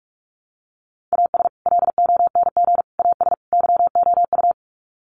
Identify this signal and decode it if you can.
Morse code, heard as a keyed tone, about 44 wpm, 700 Hz. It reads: AS LONG AS YOU
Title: Sets of 4 Words 45wpm